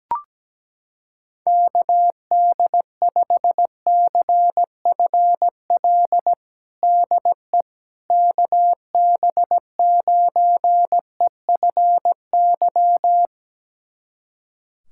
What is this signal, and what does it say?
Morse code, heard as a keyed tone, about 17 wpm, 700 Hz. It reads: KD5CFL DE KB9EFY
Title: USA Callsign DE Callsign 17wpm